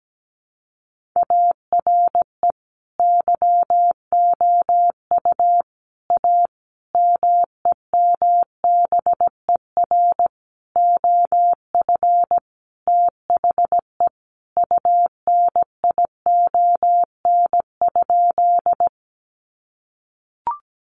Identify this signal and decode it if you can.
Morse code, heard as a keyed tone, about 17 wpm, 700 Hz. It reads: ARE YOU A MEMBER OF THE UNION?
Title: Sentences from Top 1400 Words 17wpm